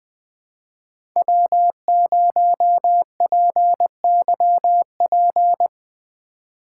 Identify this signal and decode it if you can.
Morse code, heard as a keyed tone, about 21 wpm, 700 Hz. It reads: W0PYP
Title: Speed Racing - USA 1x3 Call Signs 17wpm